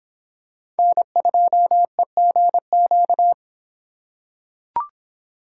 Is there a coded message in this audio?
Yes: Morse code from 0.8 to 3.3 s, heard as a keyed tone, about 26 wpm, 700 Hz. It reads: N2EGQ